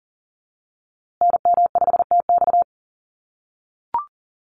Morse code, heard as a keyed tone, about 40 wpm, 700 Hz. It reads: NM5TX
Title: USA Call Signs Repeated 2x 40wpm